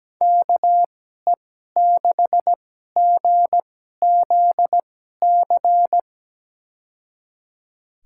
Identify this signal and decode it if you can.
Morse code, heard as a keyed tone, about 17 wpm, spaced as for 12 wpm, 700 Hz. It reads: KE6GZC
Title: USA Call Signs - 6 Characters - 2x Character Spacing - 17wpm